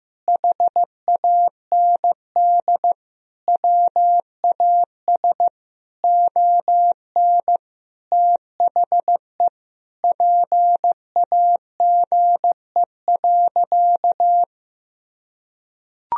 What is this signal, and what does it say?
Morse code, heard as a keyed tone, about 15 wpm, 700 Hz. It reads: HAND WAS ON THE PAGE.
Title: Sentences from Top 1000 Words - Review Part 1 - 15wpm